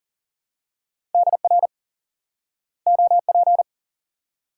Morse code, heard as a keyed tone, about 38 wpm, 700 Hz. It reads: DR OP
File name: course / instant-qso-element-course-3 / IQE Course 3 - Lesson 89 - YAGI - Two Elements - 40wpm